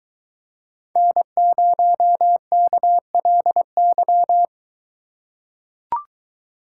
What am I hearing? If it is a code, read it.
Morse code, heard as a keyed tone, about 21 wpm, 700 Hz. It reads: N0KLY